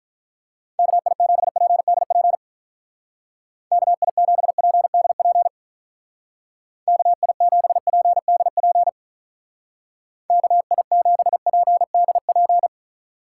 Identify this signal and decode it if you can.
Morse code, heard as a keyed tone, about 43 wpm, 700 Hz. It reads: KI7PDP KI7PDP KI7PDP KI7PDP
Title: Speed Racing - USA 2x3 Call Signs 35wpm